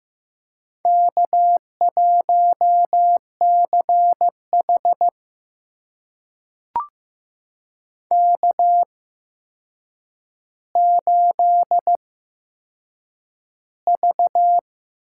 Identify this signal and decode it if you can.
Morse code, heard as a keyed tone, about 15 wpm, spaced as for 7 wpm, 700 Hz. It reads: K1CH K8V
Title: USA Call Signs - 4 Characters - 8x Character Spacing - 15wpm